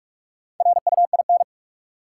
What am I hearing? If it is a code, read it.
Morse code, heard as a keyed tone, about 46 wpm, 700 Hz. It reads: RUIN
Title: Four Letter Words - 45wpm